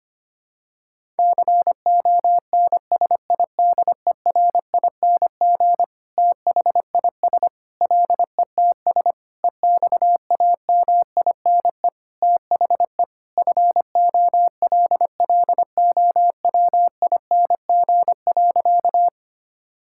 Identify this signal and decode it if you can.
Morse code, heard as a keyed tone, about 25 wpm, 700 Hz. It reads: CONSIDERING THIS LETS EXAMINE THE FOLLOWING.
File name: practice / 25wpm / Sentences with 9-Word Phrases - 25wpm